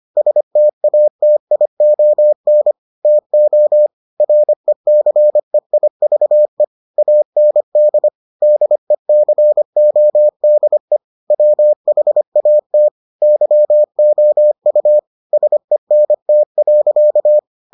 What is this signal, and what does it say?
Morse code, heard as a keyed tone, about 25 wpm, 600 Hz. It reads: STATION TO RECEIVE AND DECODE WHAT YOU SENT.